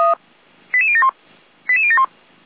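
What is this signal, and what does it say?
Five-tone selective call (ZVEI), heard as quick runs of five tones, tones of about 70 ms. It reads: 88071 88071
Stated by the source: ZVEI five-tone selective call 88149 received by SDR.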